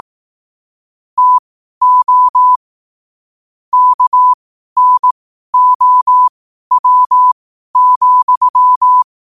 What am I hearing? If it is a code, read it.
Morse code, heard as a keyed tone, about 18 wpm, spaced as for 12 wpm, 1000 Hz. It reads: TO KNOW,
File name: qotdfiles / 12wpm